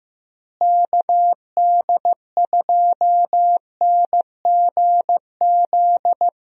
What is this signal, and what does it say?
Morse code, heard as a keyed tone, about 15 wpm, 700 Hz. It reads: KD2NGZ
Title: USA Call Signs - 6 Characters - Repeated 2x - 15wpm